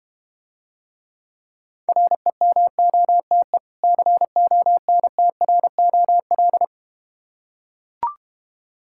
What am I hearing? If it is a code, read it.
Morse code, heard as a keyed tone, about 31 wpm, 700 Hz. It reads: REMOTE CONTROL